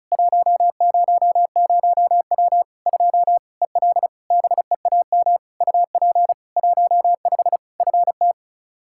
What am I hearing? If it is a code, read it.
Morse code, heard as a keyed tone, about 34 wpm, 700 Hz. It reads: 100W 2 EL BEAM UP 15 FT
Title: Practice QSO - Exchange 2 - Info - 35wpm